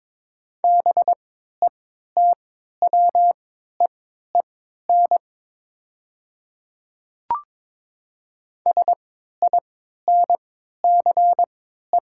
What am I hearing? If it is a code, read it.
Morse code, heard as a keyed tone, about 22 wpm, spaced as for 12 wpm, 700 Hz. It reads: BETWEEN SINCE
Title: 3x Character Spacing - Top 300 Words - 22wpm